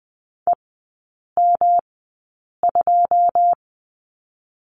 Morse code, heard as a keyed tone, about 19 wpm, 700 Hz. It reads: E M 2
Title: 107 Lesson 18e - Letter m - Three Characters 20wpm